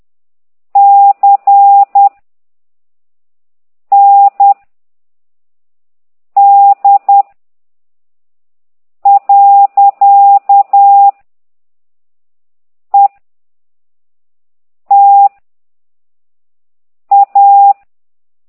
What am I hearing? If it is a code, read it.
Morse code, heard as a keyed tone, about 10 wpm, 800 Hz. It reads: C N D . E T A
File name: ct2idw / exerc9